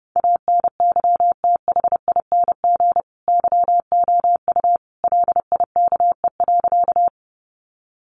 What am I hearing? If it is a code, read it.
Morse code, heard as a keyed tone, about 30 wpm, 700 Hz. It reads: ANYTHING YOU LIKE.
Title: Sentences from Top 1000 Words 30wpm